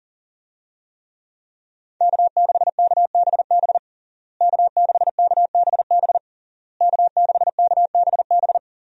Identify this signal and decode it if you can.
Morse code, heard as a keyed tone, about 40 wpm, 700 Hz. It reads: K6KBB K6KBB K6KBB
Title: USA Call Signs - 5 Characters - Repeated 3x - 40wpm